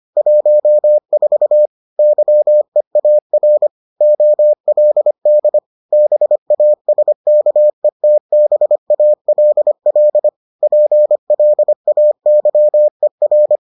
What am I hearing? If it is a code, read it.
Morse code, heard as a keyed tone, about 25 wpm, 600 Hz. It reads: 14 YEAR OLD BASKETBALL PLAYER